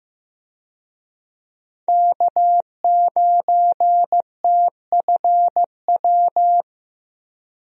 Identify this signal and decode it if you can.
Morse code, heard as a keyed tone, about 15 wpm, 700 Hz. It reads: K9TFW
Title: USA Call Signs - 5 Characters - 15wpm